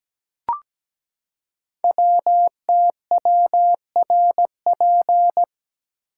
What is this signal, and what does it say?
Morse code, heard as a keyed tone, about 17 wpm, 700 Hz. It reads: WTWRP
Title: Code Groups - 5 Letters - 17wpm